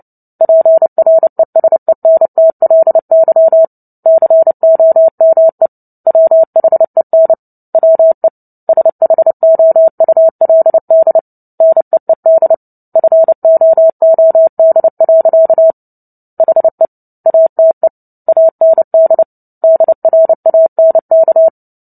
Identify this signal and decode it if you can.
Morse code, heard as a keyed tone, about 30 wpm, 650 Hz. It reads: PRESENTLY COME WHEN WE SHOULD NEED FOOD. HE ATE AND DRANK
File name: Book Two / 03-Chapter 3 - 29 WPM